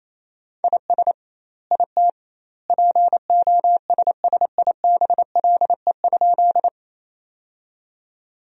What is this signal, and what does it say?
Morse code, heard as a keyed tone, about 28 wpm, 700 Hz. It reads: IS IT POSSIBLE?
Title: 2x Word-Spacing - Sentences from Top 500 Words - 28wpm